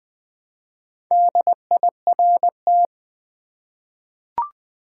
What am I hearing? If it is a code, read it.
Morse code, heard as a keyed tone, about 20 wpm, 700 Hz. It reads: DIRT
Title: Four Letter Words - 20wpm